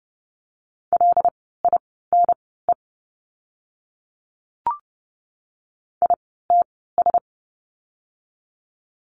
Morse code, heard as a keyed tone, about 30 wpm, spaced as for 17 wpm, 700 Hz. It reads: LINE ITS
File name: course / 30:10wpm / 115 Lesson 19f - Letter w - Words 30wpm